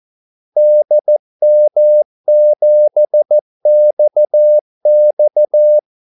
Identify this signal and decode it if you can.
Morse code, heard as a keyed tone, about 14 wpm, 600 Hz. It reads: DM7XX
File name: Callsigns-14wpm